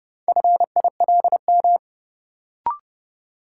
Morse code, heard as a keyed tone, about 30 wpm, 700 Hz. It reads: FILM
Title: Sentences from Top 1000 Words - Review Part 3 - New Word Drill - 30wpm